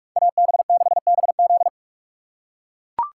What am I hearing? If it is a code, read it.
Morse code, heard as a keyed tone, about 41 wpm, 700 Hz. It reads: AB6BZ